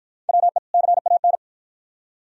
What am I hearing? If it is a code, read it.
Morse code, heard as a keyed tone, about 51 wpm, 700 Hz. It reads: WE CAN